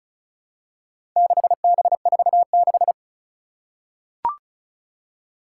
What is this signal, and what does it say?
Morse code, heard as a keyed tone, about 36 wpm, 690 Hz. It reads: 6B46
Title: Four Challenging Letter-Number Combinations 35wpm